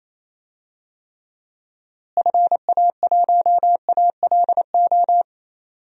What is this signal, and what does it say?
Morse code, heard as a keyed tone, about 28 wpm, 700 Hz. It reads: FA1ALO
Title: DXCC-100 Call Signs - 6 Characters - Repeated 2x - 28wpm